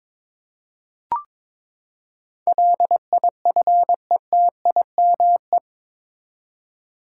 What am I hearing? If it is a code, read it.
Morse code, heard as a keyed tone, about 22 wpm, 700 Hz. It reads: LIFETIME
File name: practice / 22wpm / Eight Letter Words - 22wpm